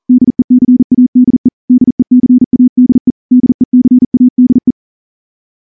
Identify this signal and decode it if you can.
Morse code, heard as a keyed tone, about 38 wpm, 250 Hz. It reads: DECADE DECADE DECADE
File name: morse / morse-training-40WPM